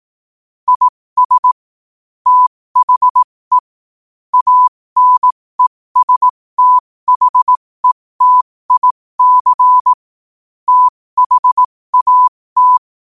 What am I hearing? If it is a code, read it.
Morse code, heard as a keyed tone, about 18 wpm, spaced as for 15 wpm, 1000 Hz. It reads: IS THE ANESTHETIC THAT
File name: 15wpm